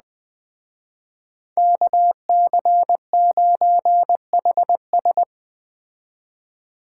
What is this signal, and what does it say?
Morse code, heard as a keyed tone, about 21 wpm, 700 Hz. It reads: KC9HS
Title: Speed Racing - USA 2x2 Call Signs 17wpm